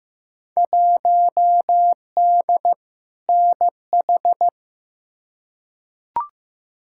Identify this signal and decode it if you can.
Morse code, heard as a keyed tone, about 15 wpm, 700 Hz. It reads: 1D NH
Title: Field Day Exchanges - 15wpm